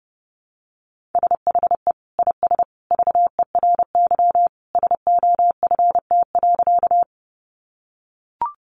Morse code, heard as a keyed tone, about 30 wpm, 700 Hz. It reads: SHE IS VERY SOFT.
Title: Easy Phrases 30wpm